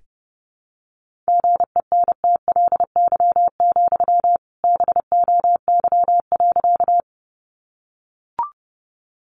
Morse code, heard as a keyed tone, about 30 wpm, 700 Hz. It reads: GENTLY, BOY.
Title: Sentences from Top 1500 Words 30wpm